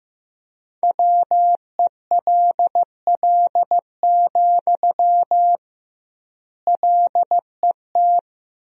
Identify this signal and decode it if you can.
Morse code, heard as a keyed tone, about 15 wpm, 700 Hz. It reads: WELL, LET
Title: 2x Word-Spacing - Sentences from Top 300 Words - 15wpm